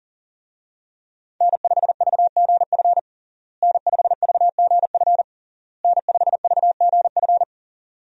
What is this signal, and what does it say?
Morse code, heard as a keyed tone, about 40 wpm, 700 Hz. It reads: N5VGF N5VGF N5VGF
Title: USA Call Signs - 5 Characters - Repeated 3x - 40wpm